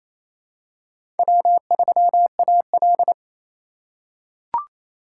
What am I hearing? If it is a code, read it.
Morse code, heard as a keyed tone, about 27 wpm, 700 Hz. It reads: W3AL
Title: USA Call Signs - 4 Characters - 3x Character Spacing - 28wpm